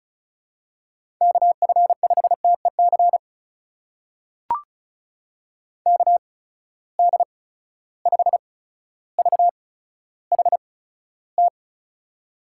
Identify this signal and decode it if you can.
Morse code, heard as a keyed tone, about 36 wpm, spaced as for 15 wpm, 700 Hz. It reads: KF5TEC KD5VHT
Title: USA Call Signs - 6 Characters - 8x Character Spacing - 35wpm